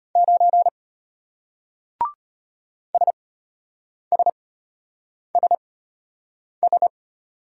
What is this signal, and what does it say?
Morse code, heard as a keyed tone, about 33 wpm, spaced as for 8 wpm, 700 Hz. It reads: 9 SSSS